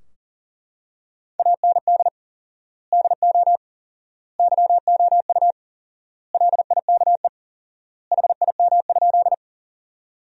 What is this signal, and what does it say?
Morse code, heard as a keyed tone, about 40 wpm, 700 Hz. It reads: AND DO YOU LIKE HIM?
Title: 4x Word-Spacing - Sentences from Top 100 Words - 40wpm